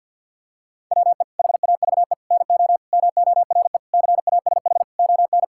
Morse code, heard as a keyed tone, about 50 wpm, 700 Hz. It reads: WE HAVE NO MORE CASH ON
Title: Sentences from Top 1000 Words - Review Part 5 - 50wpm